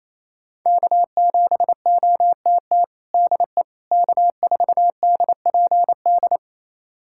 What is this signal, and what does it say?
Morse code, heard as a keyed tone, about 28 wpm, 700 Hz. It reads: K7OTT DE K4DPD
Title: USA Callsign DE Callsign - 5 Characters - 28wpm